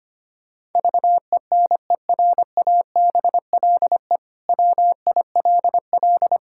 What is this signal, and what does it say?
Morse code, heard as a keyed tone, about 25 wpm, 700 Hz. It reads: VENERABLE WILL
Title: Pangram Challenge v2 - 25wpm